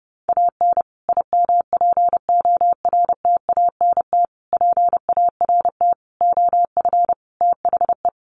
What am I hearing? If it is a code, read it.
Morse code, heard as a keyed tone, about 30 wpm, 700 Hz. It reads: AN IMPORTANT PART OF THE
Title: Sets of 5 Words Encore 30wpm